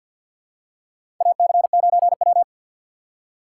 Morse code, heard as a keyed tone, about 47 wpm, 700 Hz. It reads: AC9W
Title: USA Call Signs - 4 Characters - 3x Character Spacing - 50wpm